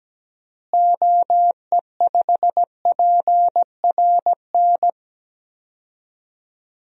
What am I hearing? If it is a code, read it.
Morse code, heard as a keyed tone, about 17 wpm, 700 Hz. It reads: OE5PRN
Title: DXCC-100 Call Signs - 6 Characters - Repeated 3x - 17wpm